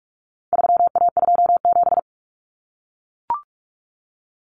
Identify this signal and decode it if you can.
Morse code, heard as a keyed tone, about 44 wpm, 700 Hz. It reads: 3A27
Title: Common Cut Numbers - 4-Digit 45wpm